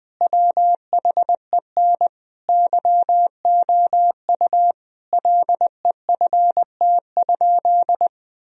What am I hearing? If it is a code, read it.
Morse code, heard as a keyed tone, about 21 wpm, 700 Hz. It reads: WHEN YOU LEFT?
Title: Sentences from Top 1000 Words - Review Part 2 - 20wpm